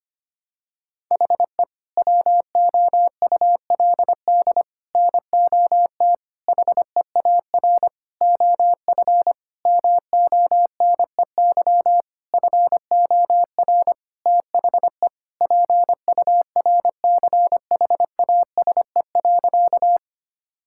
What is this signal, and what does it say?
Morse code, heard as a keyed tone, about 25 wpm, 700 Hz. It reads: HE WOULD NOT HEAR OF MONEY FOR THE PURCHASE.
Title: Sentences from Top 1000 Words - Review Part 5 - 25wpm